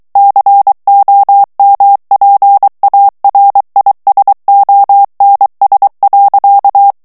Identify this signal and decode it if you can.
Morse code, heard as a keyed tone, about 23 wpm, 800 Hz. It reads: COMPARISONS.
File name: morse mp3 / ANGLAIS texte clair 2 vitesse 22 mots minute